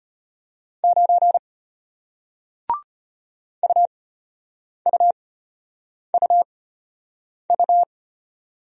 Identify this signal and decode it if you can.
Morse code, heard as a keyed tone, about 33 wpm, spaced as for 8 wpm, 700 Hz. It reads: 9 UUUU